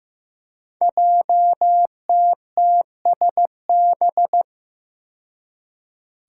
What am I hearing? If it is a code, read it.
Morse code, heard as a keyed tone, about 15 wpm, 700 Hz. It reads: JTTSB
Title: Five Challenging Letter-Number Combinations 15wpm